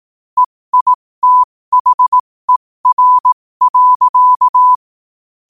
Morse code, heard as a keyed tone, about 18 wpm, spaced as for 15 wpm, 1000 Hz. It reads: EITHER.
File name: qotdfiles / 15wpm